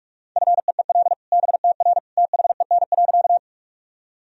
Morse code, heard as a keyed tone, about 44 wpm, 700 Hz. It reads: FEEL BTR THEN.
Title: Sentences from Top 400 Words and Abbreviations 45wpm